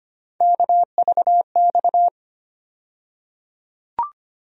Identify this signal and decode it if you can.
Morse code, heard as a keyed tone, about 25 wpm, 700 Hz. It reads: KVX
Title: USA Call Sign Postfixes - 3 Letters 25wpm